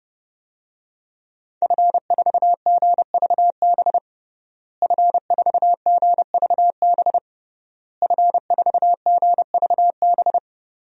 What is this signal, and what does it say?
Morse code, heard as a keyed tone, about 31 wpm, 700 Hz. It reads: F4GVB F4GVB F4GVB
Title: DXCC-100 Call Signs - 5 Characters - Repeated 3x - 30wpm